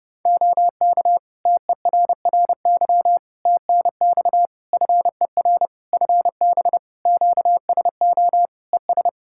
Morse code, heard as a keyed tone, about 32 wpm, 700 Hz. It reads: OK TERRY TNX FER FB QSO ES
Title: Practice QSO - Exchange 4 - Ending - 30wpm